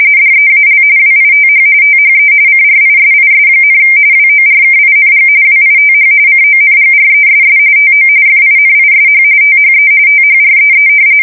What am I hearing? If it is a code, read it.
RTTY, about 45.45 baud. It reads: DEAL WITH THE CRISIS, WORKING WITH CONGRESS TO FINISH THE THREE
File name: fsk170 / usnavy1